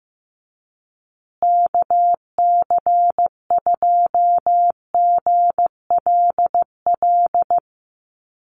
Morse code, heard as a keyed tone, about 15 wpm, 700 Hz. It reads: KC2GLL
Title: USA Call Signs Repeated 3x 15wpm